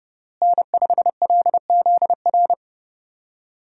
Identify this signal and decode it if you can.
Morse code, heard as a keyed tone, about 32 wpm, 700 Hz. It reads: N5LZR
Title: USA Call Signs - 5 Characters - Repeated 3x - 30wpm